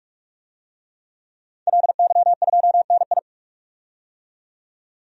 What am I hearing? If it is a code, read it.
Morse code, heard as a keyed tone, about 46 wpm, 700 Hz. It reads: LY2NI
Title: DXCC-100 Call Signs - 5 Characters - 45wpm